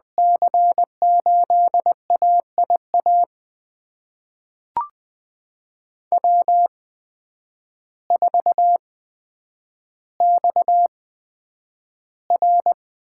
Morse code, heard as a keyed tone, about 21 wpm, spaced as for 9 wpm, 700 Hz. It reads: C8AIA W4XR
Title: USA Call Signs - 8x Character Spacing - 20wpm